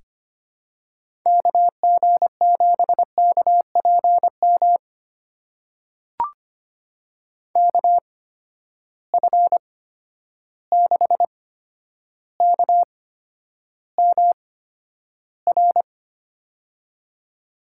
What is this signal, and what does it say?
Morse code, heard as a keyed tone, about 25 wpm, spaced as for 11 wpm, 700 Hz. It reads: KG7KPM KF6KMR
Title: USA Call Signs - 6 Characters - 8x Character Spacing - 25wpm